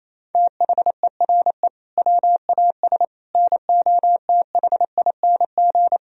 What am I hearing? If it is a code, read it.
Morse code, heard as a keyed tone, about 27 wpm, 700 Hz. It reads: THERE WAS NOTHING
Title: Sentences from Top 1000 Words - Review Part 2 - 28wpm